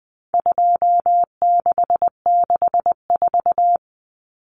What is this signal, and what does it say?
Morse code, heard as a keyed tone, about 20 wpm, 700 Hz. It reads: 2664
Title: Common Cut Numbers - 4-Digit 20wpm